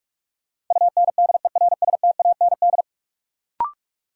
Morse code, heard as a keyed tone, about 46 wpm, 700 Hz. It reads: UNDERSTAND
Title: Sentences from Top 600 Words - New Word Drill - 45wpm